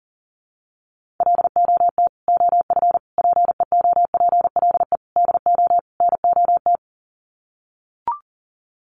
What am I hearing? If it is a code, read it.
Morse code, heard as a keyed tone, about 40 wpm, 700 Hz. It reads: LOT OF PEOPLE DO NOT
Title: Sets of 5 Words Encore 40wpm